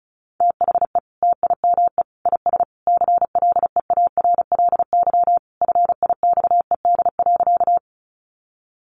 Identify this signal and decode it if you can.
Morse code, heard as a keyed tone, about 34 wpm, 700 Hz. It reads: THE TIME IS CLEARLY FIXED.
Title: Sentences from Top 1300 Words 35wpm